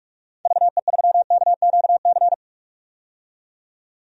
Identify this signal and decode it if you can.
Morse code, heard as a keyed tone, about 46 wpm, 700 Hz. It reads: VE3KQC
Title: DXCC-100 Call Signs - 6 Characters - 45wpm